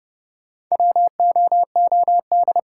Morse code, heard as a keyed tone, about 30 wpm, 700 Hz. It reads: WOOD
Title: Sentences from Top 1000 Words - Review Part 2 - New Word Drill - 30wpm